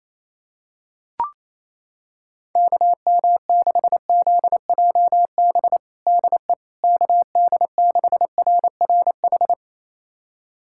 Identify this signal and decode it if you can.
Morse code, heard as a keyed tone, about 27 wpm, 700 Hz. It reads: KM6ZJB DE KD6RRH
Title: USA Callsign DE Callsign 28wpm